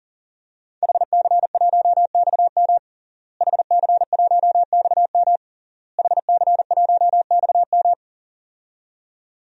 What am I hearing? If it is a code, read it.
Morse code, heard as a keyed tone, about 40 wpm, 700 Hz. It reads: HC1XM HC1XM HC1XM
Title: DXCC-100 Call Signs - 5 Characters - Repeated 3x - 40wpm